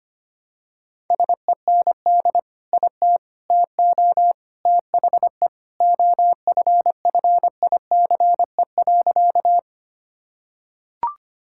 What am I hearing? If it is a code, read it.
Morse code, heard as a keyed tone, about 25 wpm, 700 Hz. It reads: SEND IT TO THE OFFICE.
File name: practice / 25wpm / Sentences with 3-Word Phrases - 25wpm